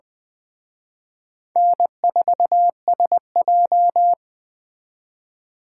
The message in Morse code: N4SJ

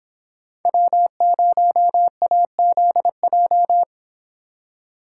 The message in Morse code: W0AZJ

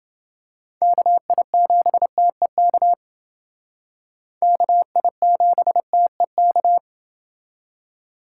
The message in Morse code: KI7TEK KI7TEK

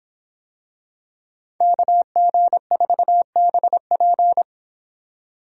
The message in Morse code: KG4BP

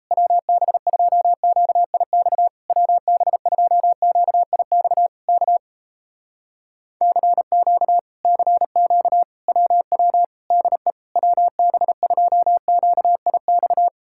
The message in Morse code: WB2QIX WB2QIX K CQ CQ WW DE WB2QIX

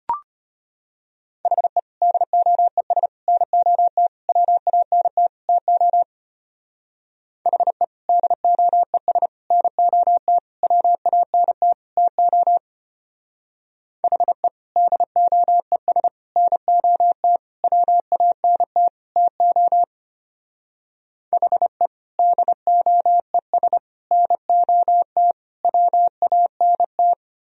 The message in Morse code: HE DOES NOT WANT TO HE DOES NOT WANT TO HE DOES NOT WANT TO HE DOES NOT WANT

